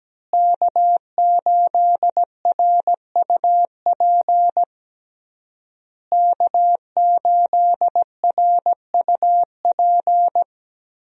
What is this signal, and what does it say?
Morse code, heard as a keyed tone, about 17 wpm, 700 Hz. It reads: K8RUP K8RUP
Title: USA Call Signs - 5 Characters - Repeated 3x - 17wpm